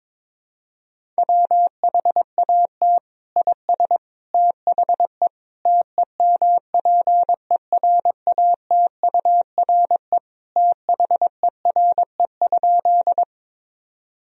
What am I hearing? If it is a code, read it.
Morse code, heard as a keyed tone, about 22 wpm, 700 Hz. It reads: WHAT IS THE TEMPERATURE THERE?
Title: Sentences from Top 1000 Words - Review Part 4 - 22wpm